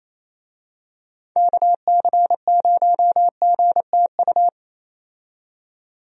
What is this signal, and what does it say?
Morse code, heard as a keyed tone, about 28 wpm, 700 Hz. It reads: KC0GTU
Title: USA Call Signs - 6 Characters - 28wpm